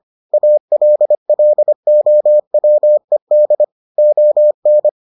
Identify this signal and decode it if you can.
Morse code, heard as a keyed tone, about 25 wpm, 600 Hz. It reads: ALLOWED ON